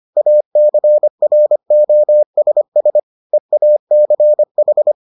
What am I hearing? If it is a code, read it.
Morse code, heard as a keyed tone, about 25 wpm, 600 Hz. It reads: ACROSS EACH